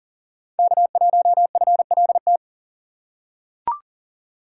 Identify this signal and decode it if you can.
Morse code, heard as a keyed tone, about 40 wpm, 700 Hz. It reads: K1FLT
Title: USA Call Signs - 5 Characters - 3x Character Spacing - 40wpm